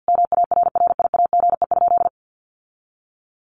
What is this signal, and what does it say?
Morse code, heard as a keyed tone, about 50 wpm, 700 Hz. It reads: MARRIAGE?
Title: Sentences from Top 1200 Words 50wpm